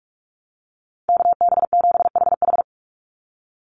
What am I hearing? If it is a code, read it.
Morse code, heard as a keyed tone, about 44 wpm, 690 Hz. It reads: KB7HH
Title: NA SOTA Chaser Callsigns 45wpm